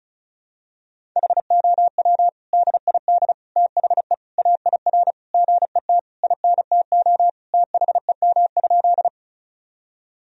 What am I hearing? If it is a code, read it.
Morse code, heard as a keyed tone, about 36 wpm, 700 Hz. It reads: HOW DID THE AIR GET INTO THEM?
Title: Sentences from Top 1000 Words - Review Part 1 - 35wpm